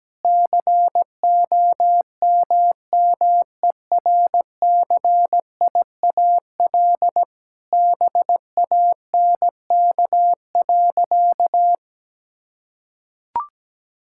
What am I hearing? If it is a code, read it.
Morse code, heard as a keyed tone, about 17 wpm, 700 Hz. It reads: COMMERCIAL BANK.
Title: Sentences from Top 1000 Words - Review Part 6 - 17wpm